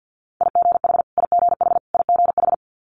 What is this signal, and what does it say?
Morse code, heard as a keyed tone, about 50 wpm, 700 Hz. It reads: IGH IGH IGH
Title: Multi-letter Phonemes Repeated 3x 50wpm